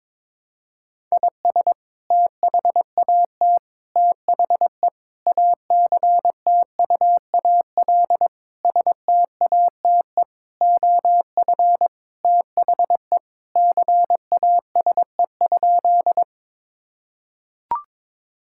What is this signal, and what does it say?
Morse code, heard as a keyed tone, about 22 wpm, 700 Hz. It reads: IS THAT THE ACTUAL STATE OF THE CASE?
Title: Sentences from Top 1000 Words - Review Part 5 - 22wpm